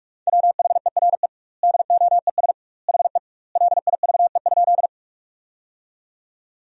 Morse code, heard as a keyed tone, about 42 wpm, 700 Hz. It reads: WHERE DOES HE LIVE?
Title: 2x Word-Spacing - Sentences from Top 200 Words - 45wpm